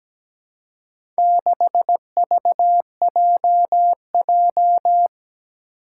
Morse code, heard as a keyed tone, about 17 wpm, 700 Hz. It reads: 6VJJ
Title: Four Challenging Letter-Number Combinations 17wpm